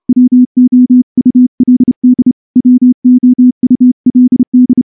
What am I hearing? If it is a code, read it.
Morse code, heard as a keyed tone, about 29 wpm, 250 Hz. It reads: WOULD WOULD
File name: morse-training-30WPM